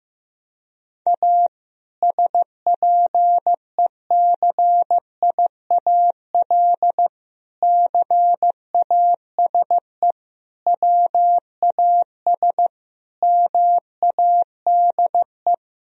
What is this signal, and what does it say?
Morse code, heard as a keyed tone, about 16 wpm, 700 Hz. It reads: A SPECIAL CASE WAS MADE
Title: Sentences from Top 1000 Words - Review Part 3 - 15wpm